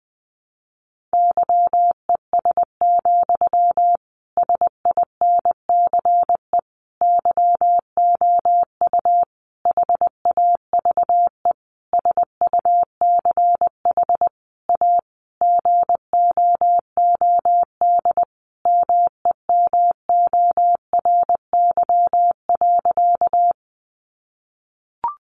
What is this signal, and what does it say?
Morse code, heard as a keyed tone, about 20 wpm, 700 Hz. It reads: YES, SINCE YOU HAVE SUCH A GOOD MEMORY.
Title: Sentences from Top 1300 Words 20wpm